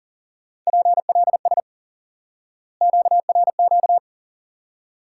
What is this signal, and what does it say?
Morse code, heard as a keyed tone, about 40 wpm, 700 Hz. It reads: PLS QRQ
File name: IQE Course 3 - Lesson 89 - YAGI - Two Elements - 40wpm